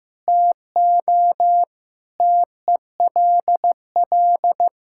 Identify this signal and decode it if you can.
Morse code, heard as a keyed tone, about 15 wpm, 700 Hz. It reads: TO TELL
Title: Sentences from Top 1600 Words 15wpm